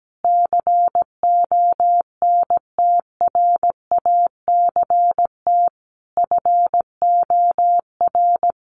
Morse code, heard as a keyed tone, about 17 wpm, 700 Hz. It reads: CONTRACT FOR
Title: Sentences from Top 1400 Words 17wpm